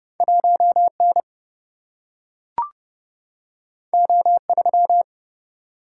Morse code, heard as a keyed tone, about 30 wpm, 700 Hz. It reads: 1N O3
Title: Single-Letter Single-Number 30wpm